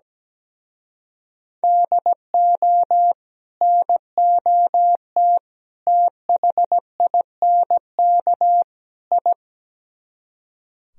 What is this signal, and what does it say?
Morse code, heard as a keyed tone, about 17 wpm, 700 Hz. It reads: DO NOT THINK I
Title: Speed-Racing - Sets of 4 Words 17wpm